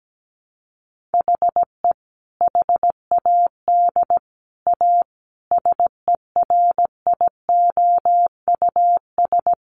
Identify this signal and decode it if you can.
Morse code, heard as a keyed tone, about 17 wpm, 700 Hz. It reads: HE HAD A SERIOUS
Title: Sentences from Top 1100 Words 17wpm